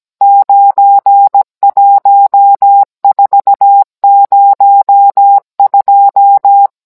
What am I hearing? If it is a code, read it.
Morse code, heard as a keyed tone, about 17 wpm, 800 Hz. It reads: 91402